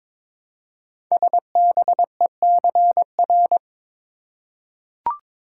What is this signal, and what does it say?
Morse code, heard as a keyed tone, about 23 wpm, 700 Hz. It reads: SBECR
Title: Code Groups - 5 Letters - 22wpm